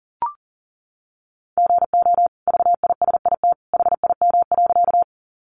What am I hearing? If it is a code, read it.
Morse code, heard as a keyed tone, about 40 wpm, 700 Hz. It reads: GO VISIT HIM.
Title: Sentences from Top 900 Words 40wpm